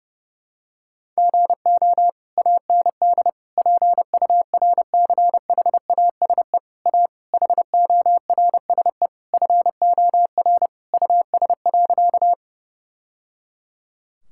Morse code, heard as a keyed tone, about 30 wpm, 700 Hz. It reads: GO AND PURCHASE A HORSE FOR US.
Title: Sentences from Top 1000 Words - Review Part 5 - 30wpm